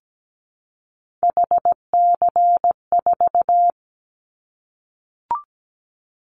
Morse code, heard as a keyed tone, about 17 wpm, 700 Hz. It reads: HC4
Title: Callsign Prefixes for Top 100 DXCC - 17wpm